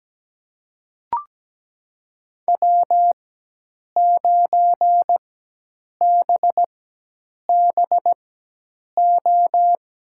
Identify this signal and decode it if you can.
Morse code, heard as a keyed tone, about 17 wpm, spaced as for 8 wpm, 700 Hz. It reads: W9BBO
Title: USA Call Signs - 5 Characters - 4x Character Spacing - 17wpm